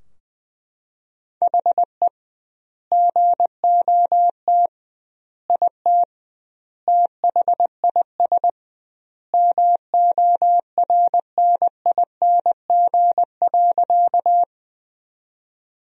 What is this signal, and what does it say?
Morse code, heard as a keyed tone, about 21 wpm, 700 Hz. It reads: HE GOT IT THIS MORNING.
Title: 2x Word-Spacing - Sentences from Top 400 Words - 20wpm